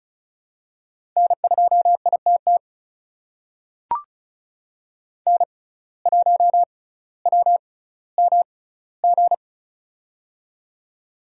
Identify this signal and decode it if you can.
Morse code, heard as a keyed tone, about 36 wpm, spaced as for 18 wpm, 700 Hz. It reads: N2ITT N1WMG
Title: USA Call Signs - 5 Characters - 6x Character Spacing - 35wpm